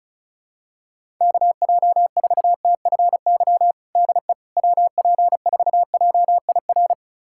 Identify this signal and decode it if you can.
Morse code, heard as a keyed tone, about 36 wpm, 700 Hz. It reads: KJ4TFY DE WP4JIR
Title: USA Callsign DE Callsign 35wpm